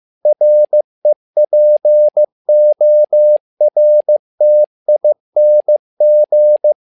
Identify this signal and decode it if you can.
Morse code, heard as a keyed tone, about 15 wpm, 600 Hz. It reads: REPORTING
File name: learning center / tower trouble / tower trouble 15wpm